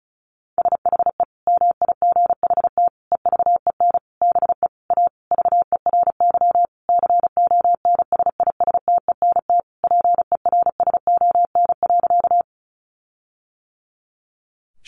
Morse code, heard as a keyed tone, about 34 wpm, 700 Hz. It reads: SHE MIGHT EVEN BE A VERY CONSISTENT PERSON.
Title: Sentences from Top 1500 Words 35wpm